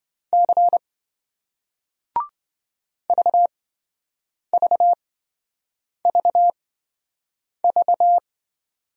Morse code, heard as a keyed tone, about 26 wpm, spaced as for 7 wpm, 700 Hz. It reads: C VVVV